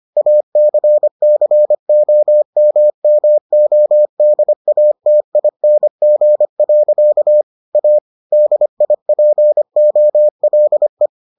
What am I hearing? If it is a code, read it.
Morse code, heard as a keyed tone, about 25 wpm, 600 Hz. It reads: ACCOMMODATING. A DIPOLE